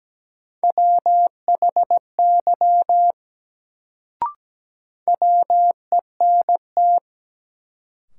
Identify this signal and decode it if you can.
Morse code, heard as a keyed tone, about 17 wpm, 700 Hz. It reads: WHY WENT
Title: Sentences from Top 200 Words - New Word Drill - 17wpm